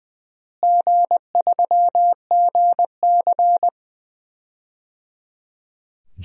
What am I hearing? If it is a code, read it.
Morse code, heard as a keyed tone, about 20 wpm, 690 Hz. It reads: G3GC
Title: DXCC-100 Call Signs - 4 Characters - Repeated 3x - 20wpm